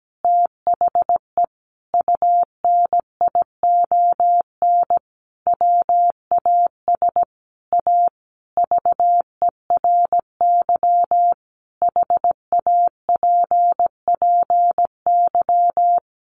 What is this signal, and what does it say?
Morse code, heard as a keyed tone, about 17 wpm, 700 Hz. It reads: THE UNION WAS A VERY HAPPY
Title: Sentences from Top 1400 Words 17wpm